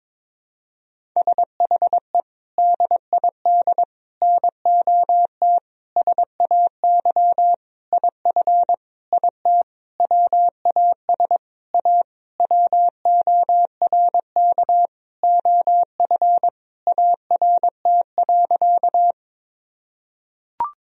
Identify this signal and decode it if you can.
Morse code, heard as a keyed tone, about 22 wpm, 700 Hz. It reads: SHE DID NOT SAY IF IT WAS A WORK OF ART.
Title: Sentences from Top 1000 Words - Review Part 3 - 22wpm